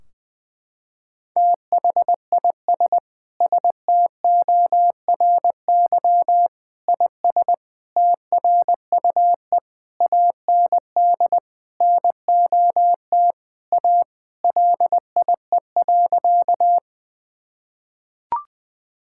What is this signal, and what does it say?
Morse code, heard as a keyed tone, about 20 wpm, 700 Hz. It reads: THIS STORY IS TRUE AND NOT A LIE.
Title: Sentences from Top 1600 Words 20wpm